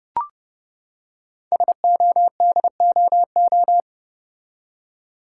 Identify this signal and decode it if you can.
Morse code, heard as a keyed tone, about 30 wpm, 700 Hz. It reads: SODOO
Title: Code Groups - 5 Letters - 30wpm